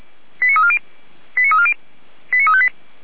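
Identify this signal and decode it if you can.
Five-tone selective call (ZVEI), heard as quick runs of five tones, tones of about 70 ms. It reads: 89340 89340 89358
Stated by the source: Received on an SDR; ZVEI five-tone call 89305, from the Feuerwehr Orlamünde (fire brigade) audio server.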